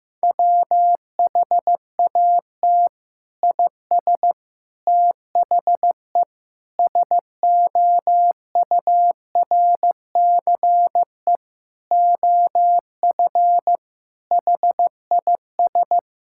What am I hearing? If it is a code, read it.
Morse code, heard as a keyed tone, about 15 wpm, 700 Hz. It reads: WHAT IS THE SOURCE OF HIS
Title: Sentences from Top 1000 Words - Review Part 5 - 15wpm